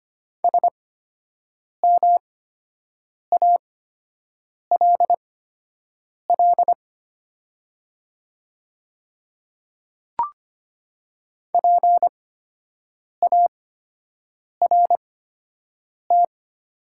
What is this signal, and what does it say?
Morse code, heard as a keyed tone, about 25 wpm, spaced as for 7 wpm, 700 Hz. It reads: SMALL PART
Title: 8x Character Spacing - Top 500 Words - 25wpm